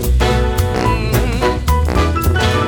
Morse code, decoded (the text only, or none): none